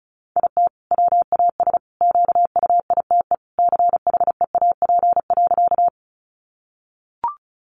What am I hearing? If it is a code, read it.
Morse code, heard as a keyed tone, about 34 wpm, 700 Hz. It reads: IT WAS QUITE CHEAP.
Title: Sentences from Top 1200 Words 35wpm